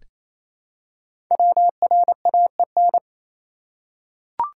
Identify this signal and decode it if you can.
Morse code, heard as a keyed tone, about 28 wpm, 700 Hz. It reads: WRAEN
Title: Code Groups - 5 Letters - 28wpm